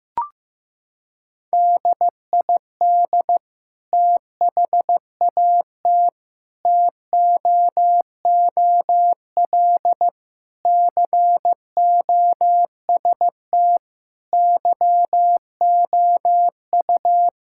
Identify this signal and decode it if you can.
Morse code, heard as a keyed tone, about 15 wpm, 700 Hz. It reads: DID THAT TOOL COST YOU
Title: Sentences from Top 1000 Words - Review Part 6 - 15wpm